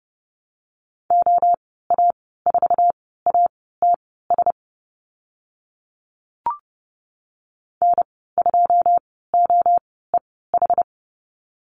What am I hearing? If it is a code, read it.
Morse code, heard as a keyed tone, about 30 wpm, spaced as for 17 wpm, 700 Hz. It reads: OA4ATS N2OEH
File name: Lesson 14g - Number 2 - Call Signs 30wpm